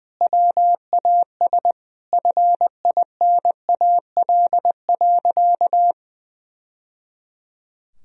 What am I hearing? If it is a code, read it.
Morse code, heard as a keyed tone, about 20 wpm, 700 Hz. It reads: WAS FINAL.
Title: Sentences from Top 1000 Words - Review Part 2 - 20wpm